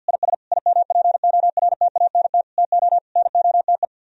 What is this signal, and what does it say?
Morse code, heard as a keyed tone, about 61 wpm, 700 Hz. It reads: IS IMPORTANT TO NOTE
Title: Speed-Racing - Sets of 5 Words 50wpm